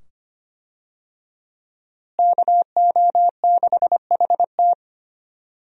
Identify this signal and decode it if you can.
Morse code, heard as a keyed tone, about 25 wpm, 700 Hz. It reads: KO6HT
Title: USA Call Signs - 5 Characters - Repeated 2x - 25wpm